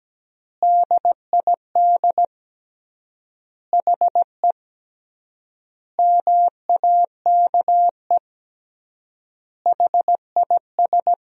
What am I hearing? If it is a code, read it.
Morse code, heard as a keyed tone, about 17 wpm, 700 Hz. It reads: DID HE MAKE HIS
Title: 3x Word-Spacing - Sentences from Top 400 Words - 17wpm